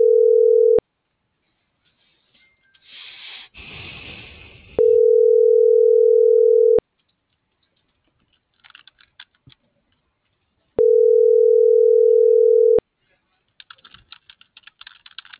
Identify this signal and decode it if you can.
Ringback tone (North America), 440 + 480 Hz.